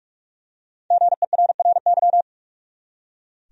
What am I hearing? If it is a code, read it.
Morse code, heard as a keyed tone, about 46 wpm, 700 Hz. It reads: GERRY
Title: USA Common Names - 45wpm